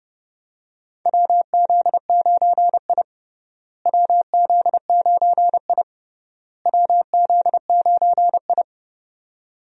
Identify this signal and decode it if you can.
Morse code, heard as a keyed tone, about 31 wpm, 700 Hz. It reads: WZ9I WZ9I WZ9I
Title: USA Call Signs - 4 Characters - Repeated 3x - 30wpm